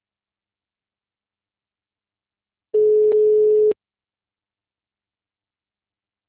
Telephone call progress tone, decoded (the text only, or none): ringback tone